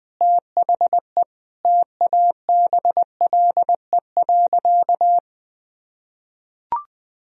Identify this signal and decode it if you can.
Morse code, heard as a keyed tone, about 20 wpm, 700 Hz. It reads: THE TABLE.
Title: Sentences from Top 1000 Words - Review Part 4 - 20wpm